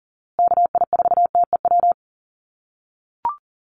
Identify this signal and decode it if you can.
Morse code, heard as a keyed tone, about 40 wpm, 700 Hz. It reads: KI4TEW
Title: USA Call Signs Repeated 3x 40wpm